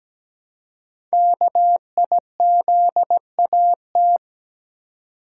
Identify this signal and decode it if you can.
Morse code, heard as a keyed tone, about 17 wpm, 700 Hz. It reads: KIZAT
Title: Code Groups - 5 Letters - 17wpm